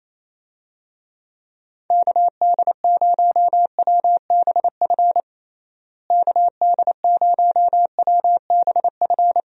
Morse code, heard as a keyed tone, about 28 wpm, 700 Hz. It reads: KD0WBF KD0WBF
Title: USA Call Signs - 6 Characters - Repeated 3x - 28wpm